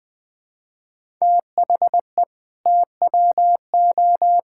Morse code, heard as a keyed tone, about 20 wpm, 700 Hz. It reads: THE TWO
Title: Sentences with 2-Word Phrases - 20wpm